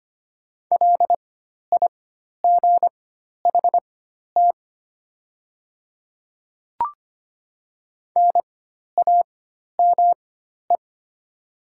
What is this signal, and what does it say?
Morse code, heard as a keyed tone, about 25 wpm, spaced as for 12 wpm, 690 Hz. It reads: LIGHT NAME
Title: 4x Character Spacing - Top 200 Words - 25wpm